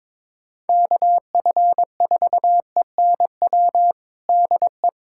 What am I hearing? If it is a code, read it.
Morse code, heard as a keyed tone, about 22 wpm, 700 Hz. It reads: KF4ENW DE
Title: USA Callsign DE Callsign - 6 Characters - 22wpm